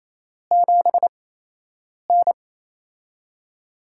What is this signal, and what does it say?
Morse code, heard as a keyed tone, about 28 wpm, 700 Hz. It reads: 7 N
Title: USA Call Signs - 4 Characters - 8x Character Spacing - 28wpm